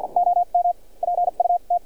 Morse code, heard as a keyed tone, about 49 wpm, spaced as for 33 wpm, 700 Hz, with radio noise. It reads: OMPUT
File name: archived / CRSnet